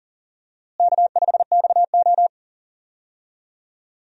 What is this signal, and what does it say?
Morse code, heard as a keyed tone, about 40 wpm, 700 Hz. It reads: K5XO
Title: USA Call Signs - 4 Characters - Repeated 2x - 40wpm